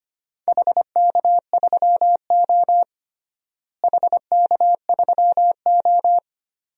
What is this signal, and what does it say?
Morse code, heard as a keyed tone, about 25 wpm, 700 Hz. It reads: HK3O HK3O
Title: DXCC-100 Call Signs - 4 Characters - Repeated 2x - 25wpm